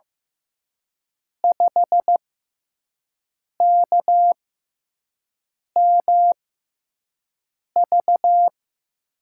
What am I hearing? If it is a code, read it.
Morse code, heard as a keyed tone, about 15 wpm, spaced as for 5 wpm, 700 Hz. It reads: 5KMV
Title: USA Call Signs - 5 Characters - 6x Character Spacing - 15wpm